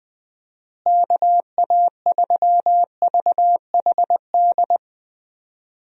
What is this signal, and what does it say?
Morse code, heard as a keyed tone, about 20 wpm, 700 Hz. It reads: KA3VHD
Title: USA Call Signs - 6 Characters - Repeated 2x - 20wpm